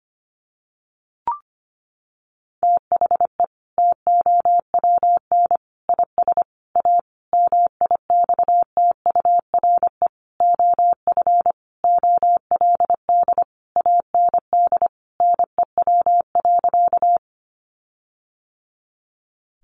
Morse code, heard as a keyed tone, about 25 wpm, 700 Hz. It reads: THE TOWN IS A MIXTURE OF OLD AND NEW.
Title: Sentences from Top 1500 Words 25wpm